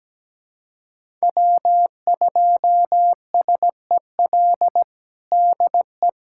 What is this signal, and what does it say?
Morse code, heard as a keyed tone, about 17 wpm, 700 Hz. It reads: W2SEL DE
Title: USA Callsign DE Callsign - 5 Characters - 17wpm